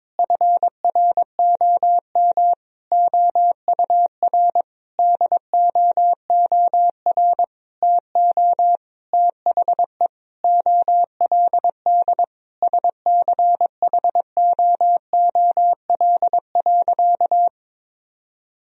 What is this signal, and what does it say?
Morse code, heard as a keyed tone, about 22 wpm, 700 Hz. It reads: FROM OUR DOOR TO THE OLD SCHOOL.
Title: Sentences from Top 1000 Words - Review Part 1 - 22wpm